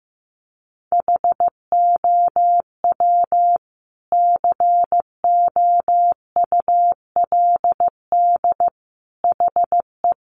Morse code, heard as a keyed tone, about 15 wpm, 700 Hz. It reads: HOW COULD HE
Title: Sentences from Top 1300 Words 15wpm